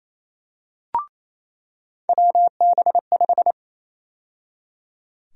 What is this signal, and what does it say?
Morse code, heard as a keyed tone, about 28 wpm, 700 Hz. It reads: WB5
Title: USA Call Sign Prefixes - 28wpm